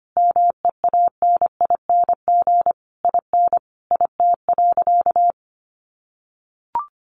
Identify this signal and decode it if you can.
Morse code, heard as a keyed tone, about 25 wpm, 700 Hz. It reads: MEANING IN IT.
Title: Sentences from Top 1200 Words 25wpm